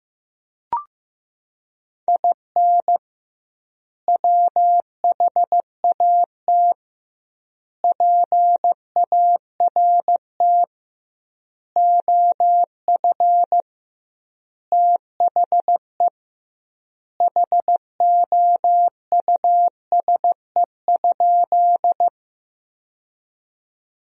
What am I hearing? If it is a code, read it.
Morse code, heard as a keyed tone, about 15 wpm, 700 Hz. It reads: IN WHAT PART OF THE HOUSE?
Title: 2x Word-Spacing - Sentences from Top 200 Words - 15wpm